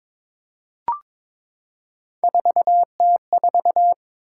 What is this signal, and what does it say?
Morse code, heard as a keyed tone, about 22 wpm, 700 Hz. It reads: 4T4
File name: practice / 22wpm / Three Letter-Number Combinations - 22wpm